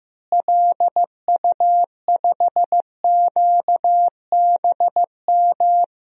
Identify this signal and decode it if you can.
Morse code, heard as a keyed tone, about 15 wpm, 700 Hz. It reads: LU5QBM
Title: DXCC-100 Call Signs - 6 Characters - Repeated 2x - 15wpm